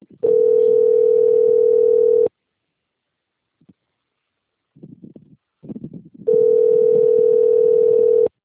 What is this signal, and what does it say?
Ringback tone (North America), 440 + 480 Hz.